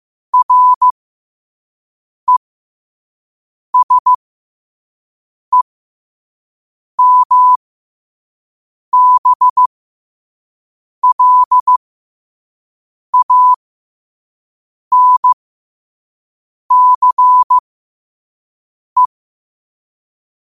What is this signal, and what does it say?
Morse code, heard as a keyed tone, about 15 wpm, spaced as for 5 wpm, 1000 Hz. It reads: RESEMBLANCE
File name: qotdfiles / 5wpm